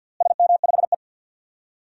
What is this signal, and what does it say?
Morse code, heard as a keyed tone, about 51 wpm, 700 Hz. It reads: SM5E
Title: DXCC-100 Call Signs - 4 Characters - Repeated 2x - 50wpm